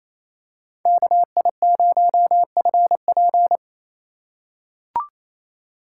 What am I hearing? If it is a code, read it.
Morse code, heard as a keyed tone, about 27 wpm, 700 Hz. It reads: KI0FP
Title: USA Call Signs - 3x Character Spacing - 28wpm